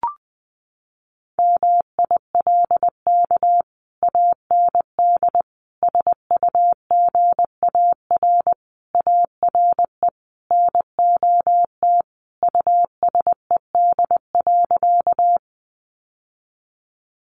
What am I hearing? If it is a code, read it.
Morse code, heard as a keyed tone, about 20 wpm, 700 Hz. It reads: MILK AND SUGAR ARE NOT USED.
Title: Sentences from Top 1500 Words 20wpm